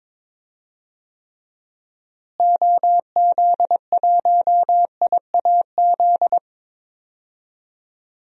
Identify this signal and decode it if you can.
Morse code, heard as a keyed tone, about 22 wpm, 700 Hz. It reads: OZ1IAZ
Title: DXCC-100 Call Signs - 6 Characters - Repeated 2x - 22wpm